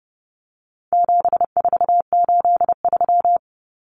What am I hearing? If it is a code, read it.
Morse code, heard as a keyed tone, about 30 wpm, 700 Hz. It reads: 7483
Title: Common Cut Numbers - 4-Digit 30wpm